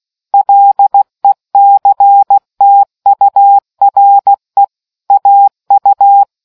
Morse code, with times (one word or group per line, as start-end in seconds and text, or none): 0.3-4.6 LECTURE
5.1-6.2 AU